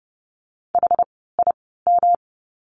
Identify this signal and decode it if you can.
Morse code, heard as a keyed tone, about 30 wpm, spaced as for 17 wpm, 700 Hz. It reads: HIM
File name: course / 30:10wpm / 108 Lesson 18f - Letter m - Words 30wpm